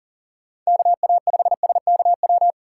Morse code, heard as a keyed tone, about 39 wpm, 700 Hz. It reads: KA5SKW
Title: USA Call Signs - 6 Characters - 4x Character Spacing - 40wpm